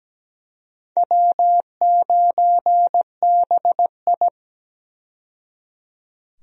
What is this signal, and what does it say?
Morse code, heard as a keyed tone, about 17 wpm, 700 Hz. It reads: W9BI
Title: USA Call Signs - 4 Characters - 17wpm